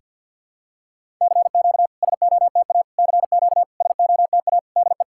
Morse code, heard as a keyed tone, about 50 wpm, 700 Hz. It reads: CQ SOTA CQ SOTA DE
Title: SOTA QSO Practice 50wpm